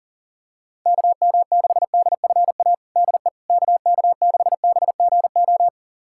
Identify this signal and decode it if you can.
Morse code, heard as a keyed tone, about 40 wpm, 700 Hz. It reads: KM6DFA DE KK6BGO
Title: USA Callsign DE Callsign - 6 Characters - 40wpm